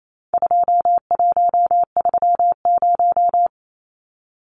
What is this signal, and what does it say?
Morse code, heard as a keyed tone, about 28 wpm, 700 Hz. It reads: 2130
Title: Numbers - 4-Digit 28wpm